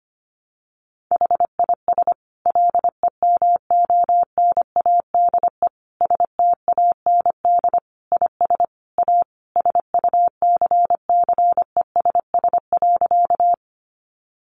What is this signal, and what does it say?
Morse code, heard as a keyed tone, about 25 wpm, 700 Hz. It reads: HIS LEMONADE STAND IS A SUCCESS.
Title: Additional Easy Phrases 25wpm